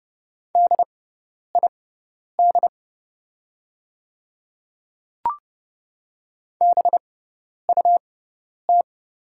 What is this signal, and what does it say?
Morse code, heard as a keyed tone, about 30 wpm, spaced as for 10 wpm, 700 Hz. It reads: DID BUT
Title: 6x Character Spacing - Top 100 Words - 30wpm